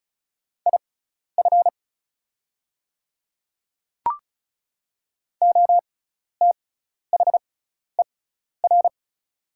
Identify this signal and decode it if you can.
Morse code, heard as a keyed tone, about 36 wpm, spaced as for 12 wpm, 690 Hz. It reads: IF OTHER
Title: 6x Character Spacing - Top 100 Words - 35wpm